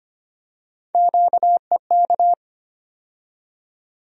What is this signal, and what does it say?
Morse code, heard as a keyed tone, about 25 wpm, 700 Hz. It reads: QEK
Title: USA Call Sign Postfixes - 3 Letters 25wpm